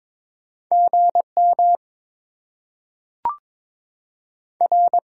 Morse code, heard as a keyed tone, about 21 wpm, 700 Hz. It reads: GM R
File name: Top 90 QSO Elements - 6x Character Spacing - 22wpm